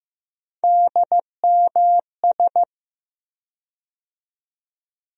Morse code, heard as a keyed tone, about 15 wpm, 700 Hz. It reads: DMS
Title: USA Call Sign Postfixes - 3 Letters 15wpm